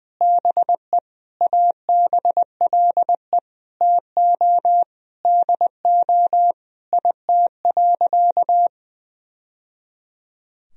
Morse code, heard as a keyed tone, about 20 wpm, 700 Hz. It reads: BE ABLE TO DO IT.
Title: Sentences from Top 1000 Words - Review Part 2 - 20wpm